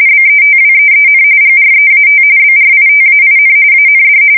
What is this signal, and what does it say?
RTTY, about 45.45 baud. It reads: BETWEEN GREAT BRITAIN AND
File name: fsk170 / usnavy1